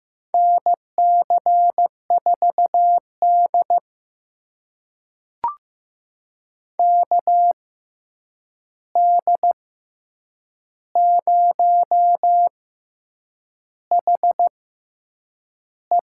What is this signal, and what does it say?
Morse code, heard as a keyed tone, about 15 wpm, spaced as for 7 wpm, 700 Hz. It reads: NC4D KD0HE
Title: USA Call Signs - 6x Character Spacing - 15wpm